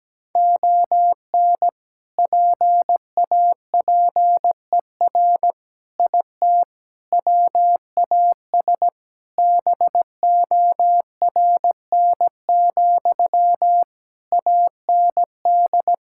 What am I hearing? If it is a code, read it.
Morse code, heard as a keyed tone, about 17 wpm, 700 Hz. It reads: ON PAPER IT WAS BORN, AND
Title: Sentences from Top 1600 Words 17wpm